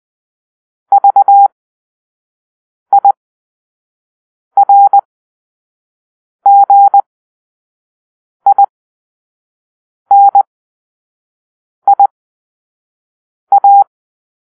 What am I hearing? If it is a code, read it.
Morse code, heard as a keyed tone, about 20 wpm, spaced as for 5 wpm, 800 Hz. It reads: VIRGINIA
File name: Grp-5 Drill-2